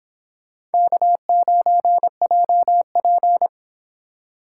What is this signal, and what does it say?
Morse code, heard as a keyed tone, about 26 wpm, 700 Hz. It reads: K9JP